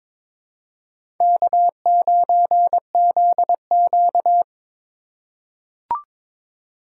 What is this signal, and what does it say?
Morse code, heard as a keyed tone, about 23 wpm, 700 Hz. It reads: K9ZQ
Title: USA Call Signs - 4 Characters - 2x Character Spacing - 22wpm